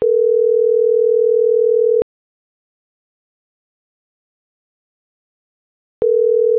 Ringback tone (North America), 440 + 480 Hz.